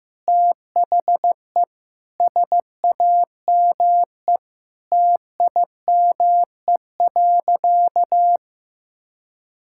Morse code, heard as a keyed tone, about 15 wpm, 700 Hz. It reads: THE SAME TIME.
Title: Sentences from Top 1000 Words - Review Part 6 - 15wpm